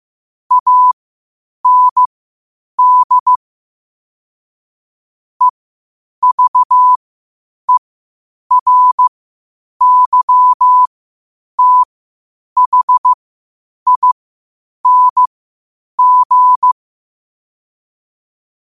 Morse code, heard as a keyed tone, about 15 wpm, spaced as for 8 wpm, 1000 Hz. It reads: AND EVERYTHING
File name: qotdfiles / 8wpm